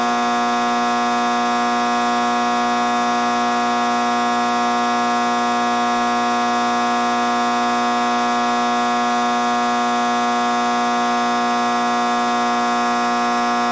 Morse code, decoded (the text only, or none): none